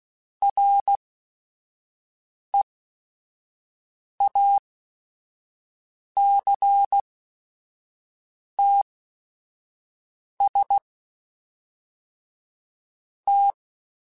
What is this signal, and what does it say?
Morse code, heard as a keyed tone, about 16 wpm, spaced as for 5 wpm, 800 Hz. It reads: REACTS T
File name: Technical Subjects / Technical Subjects-mp3s / Reactance @5